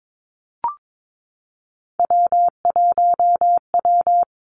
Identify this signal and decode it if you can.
Morse code, heard as a keyed tone, about 22 wpm, 700 Hz. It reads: W1W
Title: Three Challenging Letter-Number Combinations 22wpm